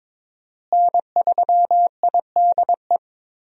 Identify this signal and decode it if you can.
Morse code, heard as a keyed tone, about 21 wpm, 700 Hz. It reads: N3IDE
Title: USA Call Signs - 5 Characters - Repeated 2x - 22wpm